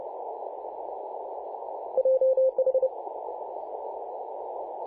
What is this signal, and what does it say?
Morse code, heard as a keyed tone, about 30 wpm, 520 Hz, with radio noise. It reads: JH
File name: c21gc-24-895mhz-cw